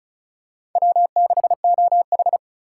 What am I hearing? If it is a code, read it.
Morse code, heard as a keyed tone, about 36 wpm, 700 Hz. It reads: W6OH
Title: USA Call Signs - 2x Character Spacing - 35wpm